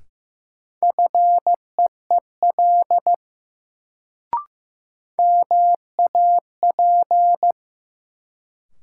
Morse code, heard as a keyed tone, about 15 wpm, 700 Hz. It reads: FEEL MAP
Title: Sentences from Top 400 Words - New Word Drill - 15wpm